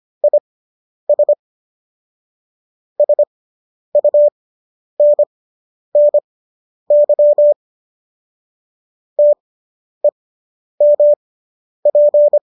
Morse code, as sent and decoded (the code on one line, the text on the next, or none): .. ... / ... ..- -. -. -.-- / - . -- .--.
IS SUNNY TEMP